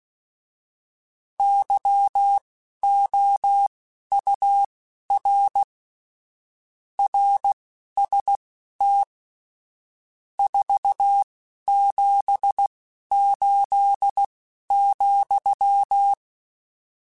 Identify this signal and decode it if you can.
Morse code, heard as a keyed tone, about 16 wpm, spaced as for 11 wpm, 800 Hz. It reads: YOUR RST 478,